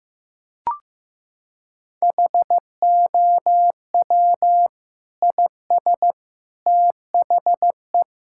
Morse code, heard as a keyed tone, about 16 wpm, 700 Hz. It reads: HOW IS THE
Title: Sentences from Top 1000 Words - Review Part 6 - 15wpm